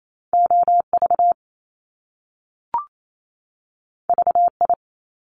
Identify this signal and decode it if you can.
Morse code, heard as a keyed tone, about 28 wpm, 700 Hz. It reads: OV VI
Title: Top Two Letter Combinations in Words 28wpm